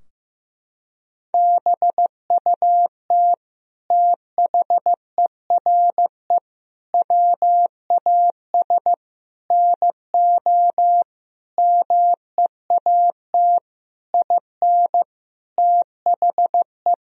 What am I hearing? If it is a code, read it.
Morse code, heard as a keyed tone, about 15 wpm, 700 Hz. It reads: BUT THERE WAS NO MEAT IN THE
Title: Sentences from Top 1000 Words - Review Part 3 - 15wpm